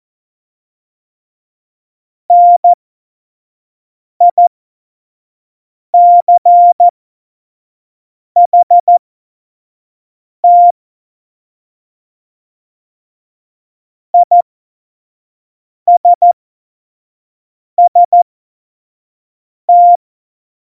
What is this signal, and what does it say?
Morse code, heard as a keyed tone, about 14 wpm, spaced as for 5 wpm, 700 Hz. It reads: NICHT ISST
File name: Morsekurs / Witze / Witz 05. Juni 2018, 25 F70